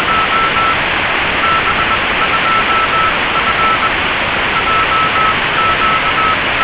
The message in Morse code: none